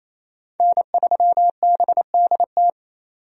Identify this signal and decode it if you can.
Morse code, heard as a keyed tone, about 28 wpm, 700 Hz. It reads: N3BDT
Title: USA Call Signs - 5 Characters - 28wpm